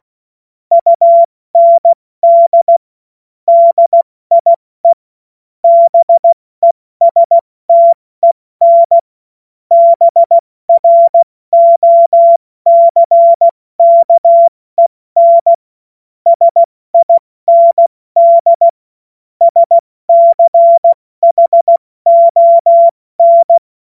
Morse code, heard as a keyed tone, about 16 wpm, 680 Hz. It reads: UND DIE BESTEN BROCKEN SIND SCHON
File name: Morsekurs / Witze / Witz 02 Januar 2024 70 F80